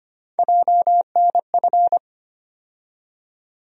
Morse code, heard as a keyed tone, about 25 wpm, 700 Hz. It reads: JNF
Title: USA Call Sign Postfixes - 3 Letters 25wpm